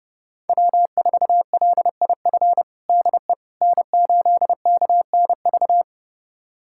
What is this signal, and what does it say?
Morse code, heard as a keyed tone, about 30 wpm, 700 Hz. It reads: W4LIF DE N8KNV
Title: USA Callsign DE Callsign - 5 Characters - 30wpm